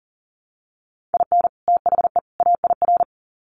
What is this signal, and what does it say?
Morse code, heard as a keyed tone, about 40 wpm, 700 Hz. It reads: IN THE AIR
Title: Sets of 3 Words 40wpm